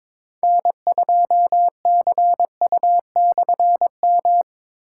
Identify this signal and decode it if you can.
Morse code, heard as a keyed tone, about 22 wpm, 700 Hz. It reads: N2CU/M
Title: K1USN SST QSO Practice - 22wpm